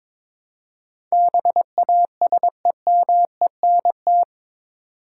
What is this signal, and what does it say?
Morse code, heard as a keyed tone, about 22 wpm, 700 Hz. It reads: BASEMENT
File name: practice / 22wpm / Eight Letter Words - 22wpm